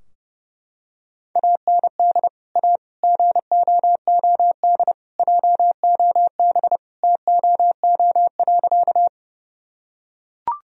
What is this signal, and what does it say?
Morse code, heard as a keyed tone, about 32 wpm, 700 Hz. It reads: AND A GOOD JOB TOO.
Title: Sentences from Top 1000 Words - Review Part 3 - 30wpm